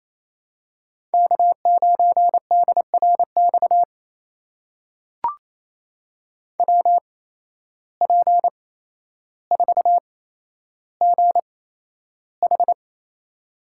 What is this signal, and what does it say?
Morse code, heard as a keyed tone, about 28 wpm, spaced as for 12 wpm, 700 Hz. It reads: K9DRX WP4GH
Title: USA Call Signs - 8x Character Spacing - 28wpm